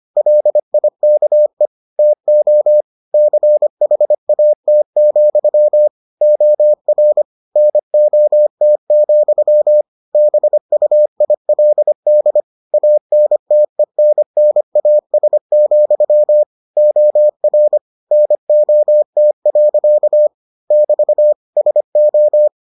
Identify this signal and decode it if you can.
Morse code, heard as a keyed tone, about 25 wpm, 600 Hz. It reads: LIKE TO CHAT, OR NOT, BUILD ANTENNAS, OR NOT. <BT> SO